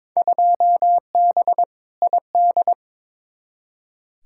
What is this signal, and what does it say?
Morse code, heard as a keyed tone, about 22 wpm, 700 Hz. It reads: 2B ID
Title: Field Day Exchanges Encore - 22wpm